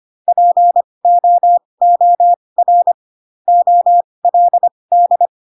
Morse code, heard as a keyed tone, about 25 wpm, spaced as for 20 wpm, 700 Hz. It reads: POOR OLD